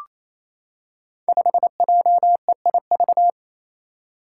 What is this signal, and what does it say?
Morse code, heard as a keyed tone, about 27 wpm, 700 Hz. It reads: 5JEIV
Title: Five Challenging Letter-Number Combinations 28wpm